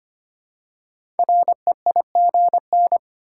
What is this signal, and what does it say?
Morse code, heard as a keyed tone, about 25 wpm, 700 Hz. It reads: REIGN
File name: 5 Letter Words - First 3 Letters Given - 25wpm